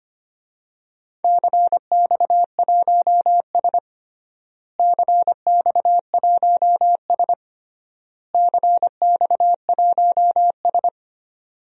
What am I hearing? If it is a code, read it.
Morse code, heard as a keyed tone, about 25 wpm, 700 Hz. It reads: CX1S CX1S CX1S
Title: DXCC-100 Call Signs - 4 Characters - Repeated 3x - 25wpm